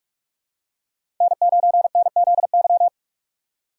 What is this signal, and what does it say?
Morse code, heard as a keyed tone, about 46 wpm, 700 Hz. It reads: N9NZY
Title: USA Call Signs - 5 Characters - 3x Character Spacing - 45wpm